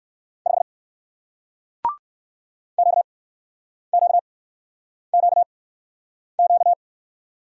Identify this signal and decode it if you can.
Morse code, heard as a keyed tone, about 59 wpm, spaced as for 9 wpm, 700 Hz. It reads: 5 QQQQ